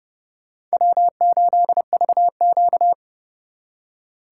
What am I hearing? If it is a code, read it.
Morse code, heard as a keyed tone, about 38 wpm, 700 Hz. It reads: W8VQ